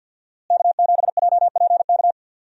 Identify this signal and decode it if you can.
Morse code, heard as a keyed tone, about 52 wpm, 700 Hz. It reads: K7JPK
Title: USA Call Signs - 5 Characters - Repeated 2x - 50wpm